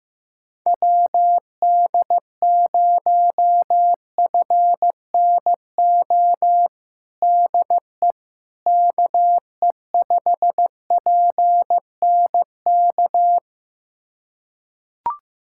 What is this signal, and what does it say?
Morse code, heard as a keyed tone, about 15 wpm, 700 Hz. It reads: WD0FNO DE KE5PNK
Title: USA Callsign DE Callsign - 6 Characters - 15wpm